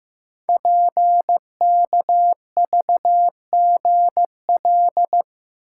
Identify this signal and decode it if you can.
Morse code, heard as a keyed tone, about 15 wpm, 700 Hz. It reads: PKVGL
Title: Code Groups - 5 Letters - 15wpm